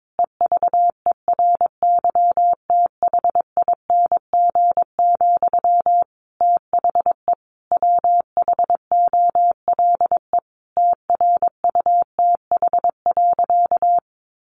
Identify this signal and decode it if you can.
Morse code, heard as a keyed tone, about 22 wpm, 700 Hz. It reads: EVERYTHING, THE WHOLE TRUTH.
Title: Sentences from Top 800 Words 22wpm